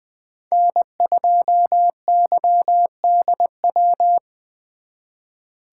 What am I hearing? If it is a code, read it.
Morse code, heard as a keyed tone, about 23 wpm, 700 Hz. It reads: N2YDW
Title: Speed Racing - USA Call Signs 20wpm